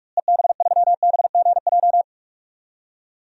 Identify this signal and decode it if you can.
Morse code, heard as a keyed tone, about 42 wpm, 700 Hz. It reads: EB3BGJ
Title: DXCC-100 Call Signs - 6 Characters - Repeated 3x - 45wpm